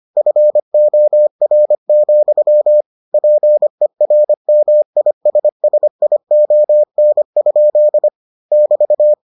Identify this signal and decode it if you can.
Morse code, heard as a keyed tone, about 25 wpm, 600 Hz. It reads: FOR, PERMISSION? <BT>